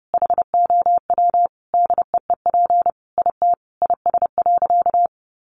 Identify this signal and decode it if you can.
Morse code, heard as a keyed tone, about 30 wpm, 700 Hz. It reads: HOW DEEP IT IS.
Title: Sentences from Top 1100 Words 30wpm